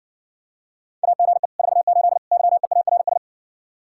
Morse code, heard as a keyed tone, about 69 wpm, 700 Hz. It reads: AGE 48 YEARS